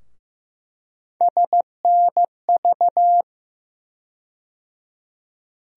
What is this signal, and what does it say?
Morse code, heard as a keyed tone, about 15 wpm, 700 Hz. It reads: SNV
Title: USA Call Sign Postfixes - 3 Letters 15wpm